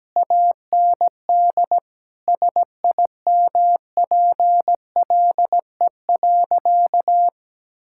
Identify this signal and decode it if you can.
Morse code, heard as a keyed tone, about 17 wpm, 700 Hz. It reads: AND SIMPLE.
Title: Sentences from Top 1000 Words - Review Part 5 - 17wpm